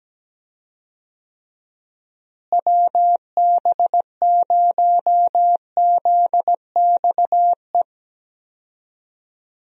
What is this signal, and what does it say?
Morse code, heard as a keyed tone, about 17 wpm, 700 Hz. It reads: WB0ZXE
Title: USA Call Signs - 6 Characters - Repeated 3x - 17wpm